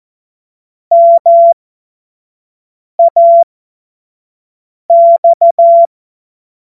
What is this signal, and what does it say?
Morse code, heard as a keyed tone, about 14 wpm, spaced as for 5 wpm, 680 Hz. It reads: MAX
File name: Morsekurs / Witze / Witz 14. Februar 2023 25 F70